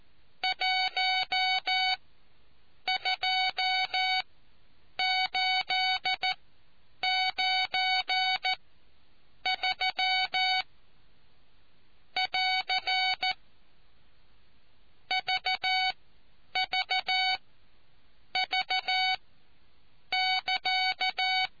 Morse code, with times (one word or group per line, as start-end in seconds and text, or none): 0.4-10.6 12893
12.2-13.3 <AR>
15.1-21.5 VVV<KA>